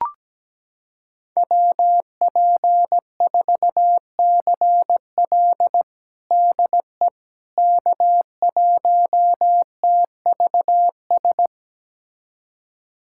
Morse code, heard as a keyed tone, about 17 wpm, 700 Hz. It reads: WP4CL DE K1TVS
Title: USA Callsign DE Callsign - 5 Characters - 17wpm